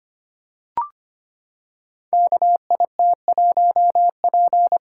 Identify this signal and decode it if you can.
Morse code, heard as a keyed tone, about 25 wpm, 700 Hz. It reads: KIT1P
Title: Code Groups - 5 Letter:Numbers - 25wpm